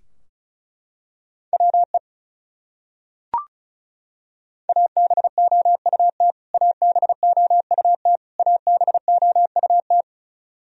Morse code, heard as a keyed tone, about 34 wpm, 700 Hz. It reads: WE ABOUT ABOUT ABOUT
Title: Top 100 Words Repeated 35wpm